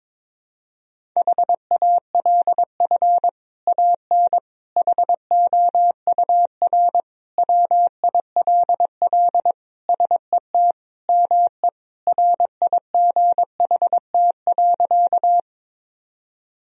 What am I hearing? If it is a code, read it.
Morse code, heard as a keyed tone, about 23 wpm, 700 Hz. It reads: HALF AN HOUR WILL SET ME RIGHT.
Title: Sentences from Top 1000 Words - Review Part 1 - 22wpm